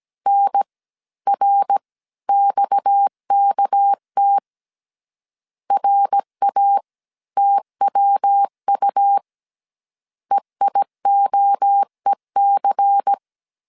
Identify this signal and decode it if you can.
Morse code, heard as a keyed tone, about 17 wpm, 800 Hz. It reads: N R XKT RA TWU EIOEC
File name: 16wpm